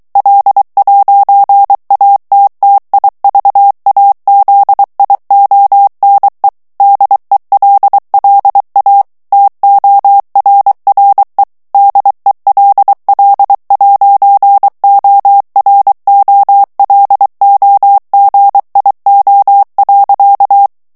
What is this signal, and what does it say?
Morse code, heard as a keyed tone, about 23 wpm, 800 Hz. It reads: L'ATTIVAZIONE DELLA TORRE DELL'OROLOGIO.
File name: morse mp3 / ITALIEN texte clair 1 vitesse 22 mots minute